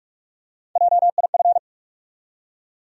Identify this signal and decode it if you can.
Morse code, heard as a keyed tone, about 46 wpm, 700 Hz. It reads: JIF
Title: USA Call Sign Postfixes - 3 Letters 45wpm